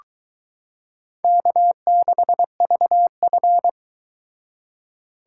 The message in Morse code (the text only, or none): K6VF